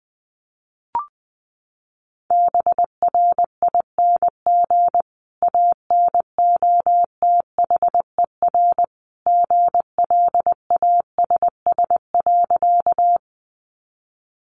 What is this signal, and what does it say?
Morse code, heard as a keyed tone, about 20 wpm, 700 Hz. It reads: BRING ANOTHER GLASS.
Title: Sentences from Top 1300 Words 20wpm